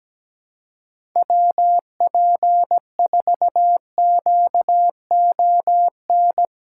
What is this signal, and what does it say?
Morse code, heard as a keyed tone, about 17 wpm, 700 Hz. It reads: WP4QON
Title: USA Callsign DE Callsign 17wpm